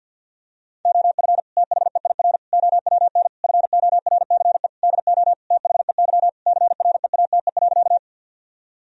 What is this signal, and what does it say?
Morse code, heard as a keyed tone, about 50 wpm, 700 Hz. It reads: OF THEIR OWN FORCE DO THEY CREATE.
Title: Sentences from Top 1000 Words - Review Part 3 - 50wpm